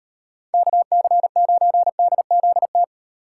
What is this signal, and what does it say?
Morse code, heard as a keyed tone, about 36 wpm, 700 Hz. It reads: KC9DZT